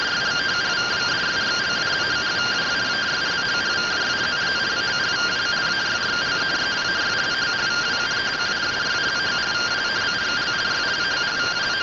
Two-tone teleprinter signal, about 170 Hz shift.